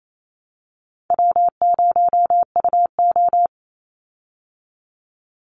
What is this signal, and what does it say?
Morse code, heard as a keyed tone, about 28 wpm, 700 Hz. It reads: W0UO
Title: Call Signs Repeated 2x - 28wpm